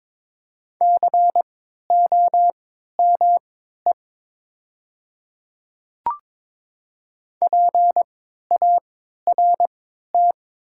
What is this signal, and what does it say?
Morse code, heard as a keyed tone, about 22 wpm, spaced as for 13 wpm, 700 Hz. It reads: COME PART
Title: 3x Character Spacing - Top 200 Words - 22wpm